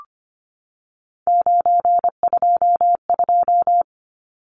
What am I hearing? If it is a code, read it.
Morse code, heard as a keyed tone, about 25 wpm, 700 Hz. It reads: 922
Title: Numbers - 3-Digit 25wpm